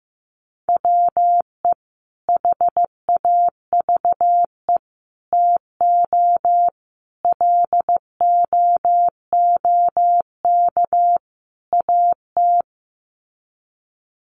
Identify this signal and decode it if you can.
Morse code, heard as a keyed tone, about 15 wpm, 700 Hz. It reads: WE HAVE TO LOOK AT
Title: Sets of 5 Words Encore2 15wpm